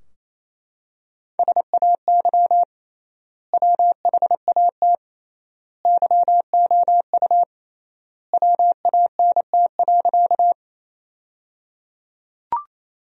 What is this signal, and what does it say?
Morse code, heard as a keyed tone, about 27 wpm, 700 Hz. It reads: SAY WHAT YOU WANT.
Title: 3x Word-Spacing - Sentences from Top 200 Words - 28wpm